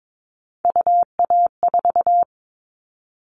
Morse code, heard as a keyed tone, about 22 wpm, 700 Hz. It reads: UA4
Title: Callsign Prefixes for Top 25 DXCC - 22wpm